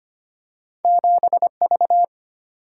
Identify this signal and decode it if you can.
Morse code, heard as a keyed tone, about 25 wpm, 700 Hz. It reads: 7V
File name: Two Letter-Number Combinations - 25wpm